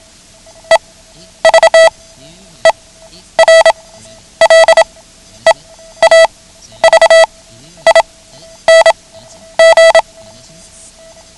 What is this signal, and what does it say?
Morse code, heard as a keyed tone, about 27 wpm, spaced as for 11 wpm, 660 Hz. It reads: EVER LEAVING
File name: ham radio / CodeQuick / Code Quick 09